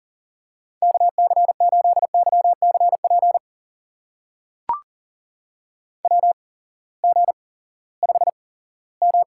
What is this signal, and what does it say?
Morse code, heard as a keyed tone, about 40 wpm, spaced as for 20 wpm, 690 Hz. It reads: KC8YCP WG5M
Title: USA Call Signs - 8x Character Spacing - 40wpm